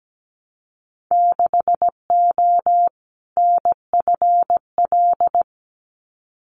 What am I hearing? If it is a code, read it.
Morse code, heard as a keyed tone, about 17 wpm, 700 Hz. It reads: 6O NFL
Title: Winter Field Day Exchanges 17wpm